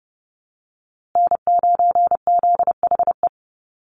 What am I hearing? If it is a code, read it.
Morse code, heard as a keyed tone, about 30 wpm, 700 Hz. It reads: N9ZHE
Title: USA Call Signs Repeated 2x 30wpm